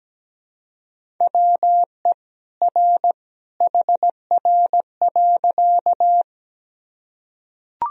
Morse code, heard as a keyed tone, about 17 wpm, 700 Hz. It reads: WE R HR.
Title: Sentences from Top 200 Words and Abbreviations 17wpm